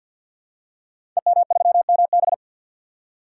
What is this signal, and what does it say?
Morse code, heard as a keyed tone, about 49 wpm, 700 Hz. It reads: EM3MB
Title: Code Groups - 5 Letter:Numbers - 50wpm